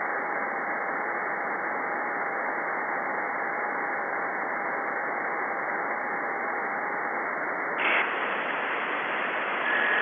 FT8.
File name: contests / 2019-june=vhf-k5tr- -0101 0-2